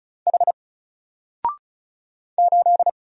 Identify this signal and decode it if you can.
Morse code, heard as a keyed tone, about 34 wpm, 700 Hz. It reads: H 8
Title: Single Letter-Number Alt 35wpm